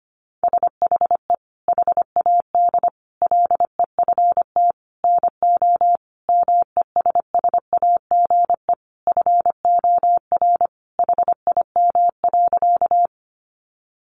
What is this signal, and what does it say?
Morse code, heard as a keyed tone, about 25 wpm, 700 Hz. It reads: SHE HAD LEFT NO MESSAGE FOR HIM.
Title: Sentences from Top 1100 Words 25wpm